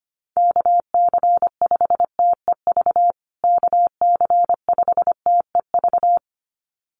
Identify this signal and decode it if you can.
Morse code, heard as a keyed tone, about 25 wpm, 700 Hz. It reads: KC5TEV KC5TEV
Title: USA Call Signs Repeated 2x 25wpm